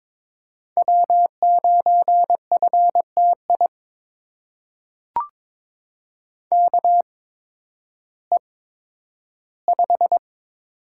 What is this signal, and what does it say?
Morse code, heard as a keyed tone, about 22 wpm, spaced as for 12 wpm, 700 Hz. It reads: W9FTI KE5
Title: USA Call Signs - 8x Character Spacing - 22wpm